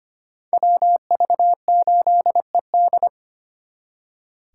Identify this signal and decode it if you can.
Morse code, heard as a keyed tone, about 32 wpm, 700 Hz. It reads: WV8ED